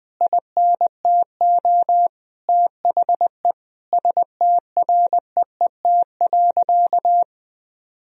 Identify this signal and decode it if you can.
Morse code, heard as a keyed tone, about 20 wpm, 700 Hz. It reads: INTO THE STREET.
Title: Sentences from Top 1000 Words - Review Part 3 - 20wpm